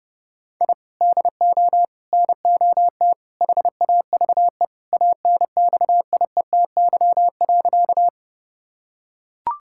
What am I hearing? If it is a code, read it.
Morse code, heard as a keyed tone, about 30 wpm, 700 Hz. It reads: I DO NOT HAVE ANXIETY.
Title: Sentences from Top 1600 Words 30wpm